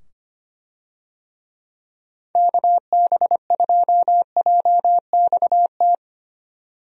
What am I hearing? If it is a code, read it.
Morse code, heard as a keyed tone, about 25 wpm, 700 Hz. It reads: KB2JXT
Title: USA Call Signs - 6 Characters - 25wpm